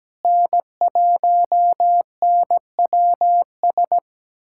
Morse code, heard as a keyed tone, about 17 wpm, 700 Hz. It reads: N1NWS
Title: USA Callsign DE Callsign - 5 Characters - 17wpm